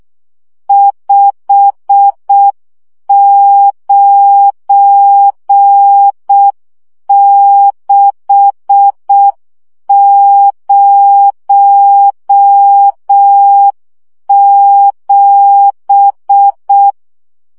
Morse code, heard as a keyed tone, about 6 wpm, 800 Hz. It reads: 59607
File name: morze vizsga 2011-1 5